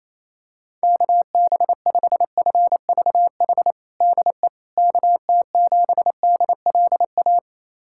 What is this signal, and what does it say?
Morse code, heard as a keyed tone, about 28 wpm, 700 Hz. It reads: KB5FVH DE KT7DLA
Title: USA Callsign DE Callsign 28wpm